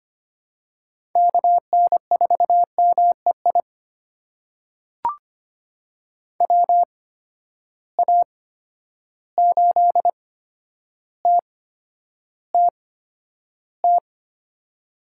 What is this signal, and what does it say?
Morse code, heard as a keyed tone, about 25 wpm, spaced as for 11 wpm, 700 Hz. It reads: KN4MEI WA8TTT
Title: USA Call Signs - 6 Characters - 8x Character Spacing - 25wpm